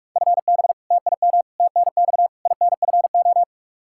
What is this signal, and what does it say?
Morse code, heard as a keyed tone, about 46 wpm, 700 Hz. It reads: FB TIM TNX INFO
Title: Practice QSO - Exchange 3 - More Info - 45wpm